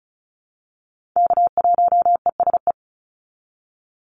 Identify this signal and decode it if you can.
Morse code, heard as a keyed tone, about 36 wpm, 700 Hz. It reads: K1ESE
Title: Call Signs - 35wpm